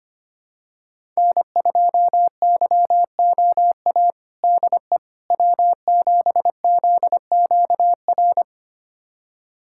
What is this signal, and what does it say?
Morse code, heard as a keyed tone, about 25 wpm, 700 Hz. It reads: N2YOA DE W7ZQR
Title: USA Callsign DE Callsign - 5 Characters - 25wpm